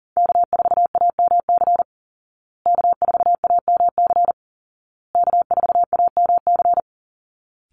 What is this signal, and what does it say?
Morse code, heard as a keyed tone, about 40 wpm, 700 Hz. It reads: K4AMC K4AMC K4AMC
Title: Call Signs Repeated 2x - 40wpm